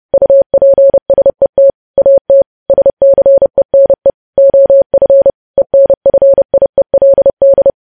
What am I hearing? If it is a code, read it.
Morse code, heard as a keyed tone, about 30 wpm, 550 Hz. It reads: UPSET AT SCENE OF ENFIELD
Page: news headlines in morse code at 30 wpm